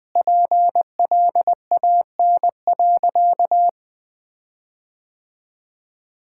Morse code, heard as a keyed tone, about 20 wpm, 700 Hz. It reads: PLAN.
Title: Sentences with 8-Word Phrases - 20wpm